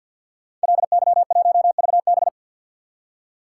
Morse code, heard as a keyed tone, about 53 wpm, 700 Hz. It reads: LY1VB
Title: Code Groups - 5 Letter:Numbers - 50wpm